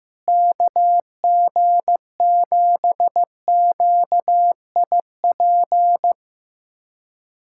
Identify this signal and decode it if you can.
Morse code, heard as a keyed tone, about 15 wpm, 700 Hz. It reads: KG7QIP
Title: USA Callsign DE Callsign 15wpm